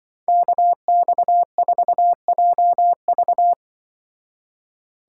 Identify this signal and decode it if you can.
Morse code, heard as a keyed tone, about 24 wpm, 700 Hz. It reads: KX4JV
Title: Speed Racing - USA 2x2 Call Signs 20wpm